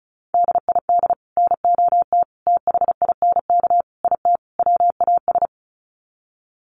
Morse code, heard as a keyed tone, about 34 wpm, 700 Hz. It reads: DID NOT THINK IT WAS
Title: Sets of 5 Words Encore 35wpm